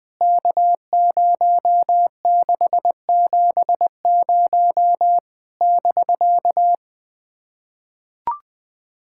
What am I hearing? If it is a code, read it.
Morse code, heard as a keyed tone, about 19 wpm, 700 Hz. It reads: K0670 <BK>
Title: POTA QSO Practice - 20wpm